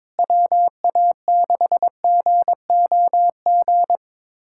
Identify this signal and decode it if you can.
Morse code, heard as a keyed tone, about 21 wpm, 700 Hz. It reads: WA6GOG
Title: USA Callsign DE Callsign 22wpm